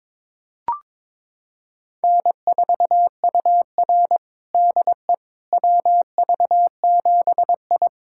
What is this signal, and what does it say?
Morse code, heard as a keyed tone, about 22 wpm, 700 Hz. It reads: N4UR DE WV7I
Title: USA Callsign DE Callsign - 4 Characters - 22wpm